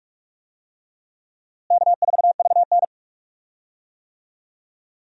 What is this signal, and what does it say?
Morse code, heard as a keyed tone, about 42 wpm, 700 Hz. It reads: K4VN
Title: USA Call Signs - 4 Characters - Repeated 2x - 45wpm